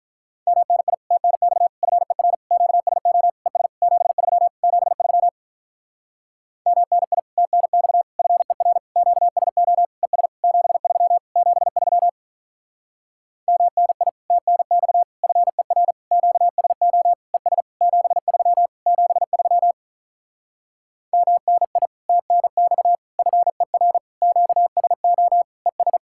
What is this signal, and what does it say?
Morse code, heard as a keyed tone, about 46 wpm, 700 Hz. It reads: MNI TNX FER QSO ES 73 73 MNI TNX FER QSO ES 73 73 MNI TNX FER QSO ES 73 73 MNI TNX FER QSO ES